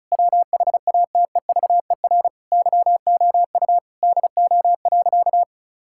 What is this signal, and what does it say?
Morse code, heard as a keyed tone, about 36 wpm, 700 Hz. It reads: WHATEVER YOU DO.
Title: Sentences from Top 1600 Words 35wpm